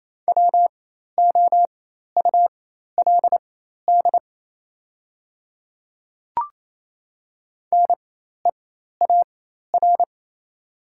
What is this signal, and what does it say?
Morse code, heard as a keyed tone, about 28 wpm, spaced as for 13 wpm, 700 Hz. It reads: WOULD NEAR
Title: 4x Character Spacing - Top 200 Words - 28wpm